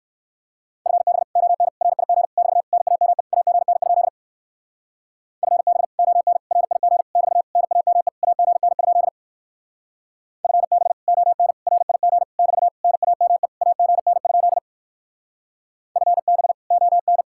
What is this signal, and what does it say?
Morse code, heard as a keyed tone, about 56 wpm, 700 Hz. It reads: FB ON RIG <BT> NAME AGN? FB ON RIG <BT> NAME AGN? FB ON RIG <BT> NAME AGN? FB ON
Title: Speed-Racing Common QSO Phrases Encore - 45wpm